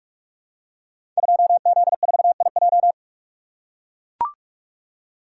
Morse code, heard as a keyed tone, about 44 wpm, 700 Hz. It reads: 2Z4IJ
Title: Code Groups - 5 Letter:Numbers - 45wpm